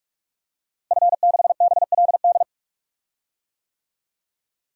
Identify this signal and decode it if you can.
Morse code, heard as a keyed tone, about 46 wpm, 700 Hz. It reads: F6BLD
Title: DXCC-100 Call Signs - 5 Characters - Repeated 2x - 45wpm